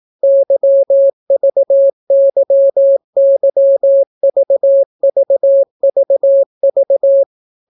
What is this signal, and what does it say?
Morse code, heard as a keyed tone, about 18 wpm, 550 Hz. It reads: YVYYVVVV